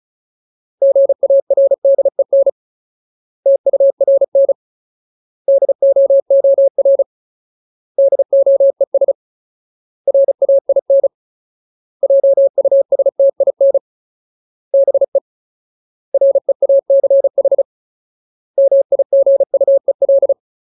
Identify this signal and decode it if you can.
Morse code, heard as a keyed tone, about 34 wpm, 550 Hz. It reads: GARDEN TURN DOOR DOES RAIN JUSTIN BE REACH MIGUEL
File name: Ultimate Word List / 35WPM Ultimate